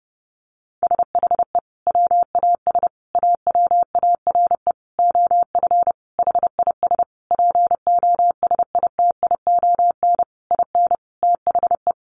Morse code, heard as a keyed tone, about 28 wpm, 700 Hz. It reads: SHE WAS AWARE OF HIS POSITION IN THE
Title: Sentences from Top 1200 Words 30wpm